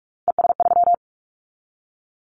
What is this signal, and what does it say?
Morse code, heard as a keyed tone, about 46 wpm, 700 Hz. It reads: ES3
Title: Callsign Prefixes for Top 100 DXCC - 45wpm